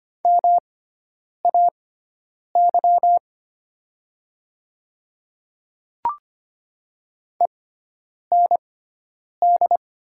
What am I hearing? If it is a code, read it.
Morse code, heard as a keyed tone, about 25 wpm, spaced as for 9 wpm, 700 Hz. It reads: MAY END
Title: 6x Character Spacing - Top 300 Words - 25wpm